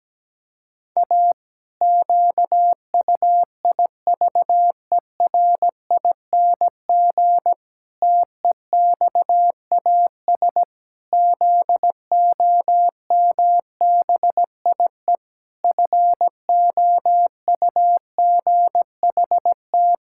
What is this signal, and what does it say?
Morse code, heard as a keyed tone, about 17 wpm, 700 Hz. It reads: A QUIVERING TEXAS ZOMBIE FOUGHT
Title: Pangram Challenge v2 - 17wpm